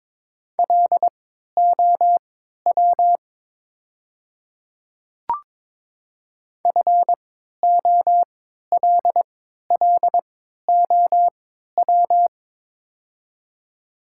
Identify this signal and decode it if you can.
Morse code, heard as a keyed tone, about 22 wpm, spaced as for 12 wpm, 700 Hz. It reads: LOW FOLLOW
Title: 3x Character Spacing - Top 500 Words - 22wpm